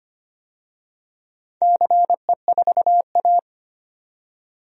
Morse code, heard as a keyed tone, about 25 wpm, 700 Hz. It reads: CE4A
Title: DXCC-100 Call Signs - 4 Characters - 25wpm